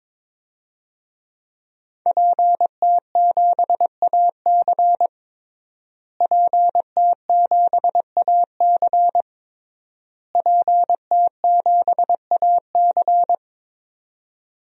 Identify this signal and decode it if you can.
Morse code, heard as a keyed tone, about 22 wpm, 700 Hz. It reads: PT7AC PT7AC PT7AC
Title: DXCC-100 Call Signs - 5 Characters - Repeated 3x - 22wpm